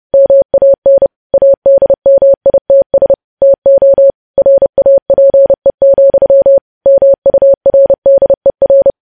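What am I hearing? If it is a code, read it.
Morse code, heard as a keyed tone, about 30 wpm, 550 Hz. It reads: MAN ADMITS TO RAPE, MURDER
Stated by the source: Morse code news headlines practice at 30 WPM.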